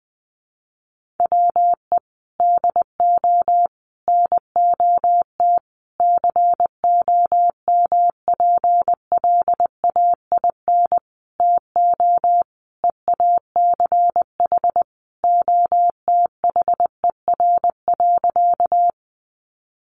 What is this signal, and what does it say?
Morse code, heard as a keyed tone, about 20 wpm, 700 Hz. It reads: WE DO NOT COMPLAIN TO EACH OTHER.
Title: Sentences from Top 1300 Words 20wpm